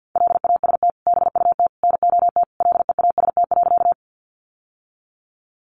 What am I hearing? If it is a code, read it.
Morse code, heard as a keyed tone, about 48 wpm, 700 Hz. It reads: LAST BUT NOT LEAST.
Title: Sentences from Top 700 Words 50wpm